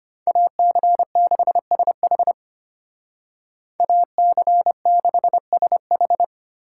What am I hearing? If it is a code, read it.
Morse code, heard as a keyed tone, about 33 wpm, 700 Hz. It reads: AC6SH AC6SH